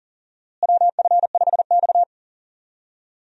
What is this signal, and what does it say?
Morse code, heard as a keyed tone, about 40 wpm, 700 Hz. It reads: WF5X
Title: USA Call Signs - 4 Characters - Repeated 3x - 40wpm